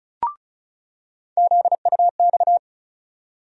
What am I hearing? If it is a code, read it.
Morse code, heard as a keyed tone, about 36 wpm, 700 Hz. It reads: ZUX
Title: USA Call Sign Postfixes - 3 Letters 35wpm